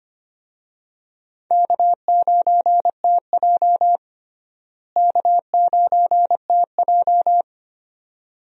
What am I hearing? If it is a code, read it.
Morse code, heard as a keyed tone, about 25 wpm, 700 Hz. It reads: K9TJ K9TJ
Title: USA Call Signs - 4 Characters - Repeated 2x - 25wpm